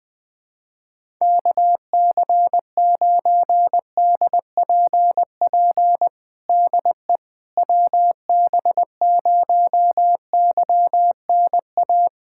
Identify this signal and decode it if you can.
Morse code, heard as a keyed tone, about 20 wpm, 700 Hz. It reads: KC9DPP DE WB0YNA
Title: USA Callsign DE Callsign - 6 Characters - 20wpm